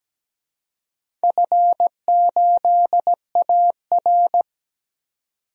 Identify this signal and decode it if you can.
Morse code, heard as a keyed tone, about 17 wpm, 700 Hz. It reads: F8AR
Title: DXCC-100 Call Signs - 4 Characters - 17wpm